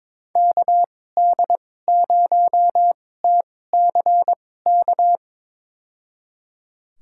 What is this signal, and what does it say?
Morse code, heard as a keyed tone, about 22 wpm, spaced as for 16 wpm, 700 Hz. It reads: KD0TCK
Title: USA Call Signs - 6 Characters - 2x Character Spacing - 22wpm